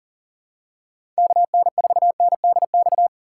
Morse code, heard as a keyed tone, about 40 wpm, 700 Hz. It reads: KN4NDX
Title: USA Call Signs - 6 Characters - 6x Character Spacing - 40wpm